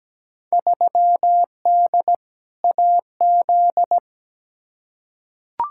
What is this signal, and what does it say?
Morse code, heard as a keyed tone, about 17 wpm, 700 Hz. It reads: 3D AZ
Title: Field Day Exchanges - 17wpm